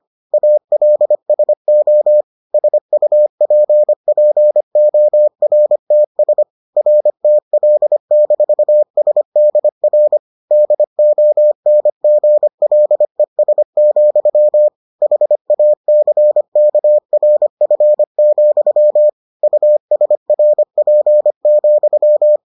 Morse code, heard as a keyed tone, about 25 wpm, 600 Hz. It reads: ALSO SUPPORTS RTL-SDR DONGLES, HACKRF, USRP,